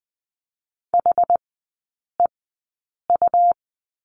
Morse code, heard as a keyed tone, about 20 wpm, spaced as for 8 wpm, 700 Hz. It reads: HEU
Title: 178 Lesson 29e - Slash - Three Characters 20wpm